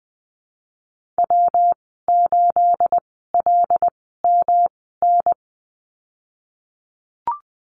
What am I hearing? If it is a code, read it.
Morse code, heard as a keyed tone, about 20 wpm, spaced as for 15 wpm, 700 Hz. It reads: W8LMN
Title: 238 Lesson 38g - Letter q - Call Signs 20wpm